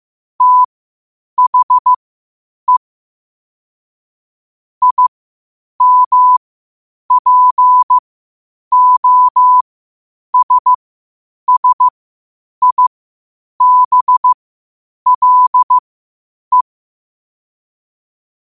Morse code, heard as a keyed tone, about 15 wpm, spaced as for 8 wpm, 1000 Hz. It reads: THE IMPOSSIBLE
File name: qotdfiles / 8wpm